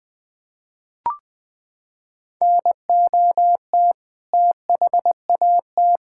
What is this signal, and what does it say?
Morse code, heard as a keyed tone, about 20 wpm, 700 Hz. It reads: NOT THAT
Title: Sentences with 7-Word Phrases - 20wpm